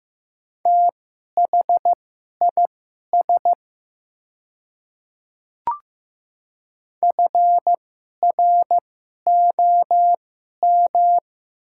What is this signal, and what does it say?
Morse code, heard as a keyed tone, about 15 wpm, spaced as for 11 wpm, 700 Hz. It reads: THIS FROM
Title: 2x Character Spacing - Top 200 Words - 15wpm